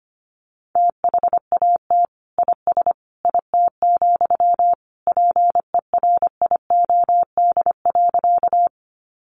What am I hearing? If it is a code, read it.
Morse code, heard as a keyed tone, about 25 wpm, 700 Hz. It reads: THAT IS IT, PERIOD.
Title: Sentences from Top 800 Words 25wpm